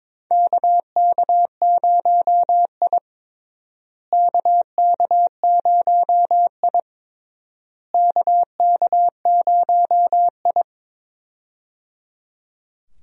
Morse code, heard as a keyed tone, about 22 wpm, 700 Hz. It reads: KK0I KK0I KK0I
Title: USA Call Signs - 4 Characters - Repeated 3x - 22wpm